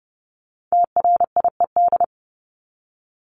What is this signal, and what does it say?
Morse code, heard as a keyed tone, about 30 wpm, 700 Hz. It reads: TRIED
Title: ED Suffix - Top 100 Words with Base Words 30wpm